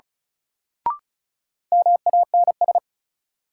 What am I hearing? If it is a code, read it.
Morse code, heard as a keyed tone, about 36 wpm, 700 Hz. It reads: MANS
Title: Sentences from Top 1000 Words - Review Part 6 - New Word Drill - 35wpm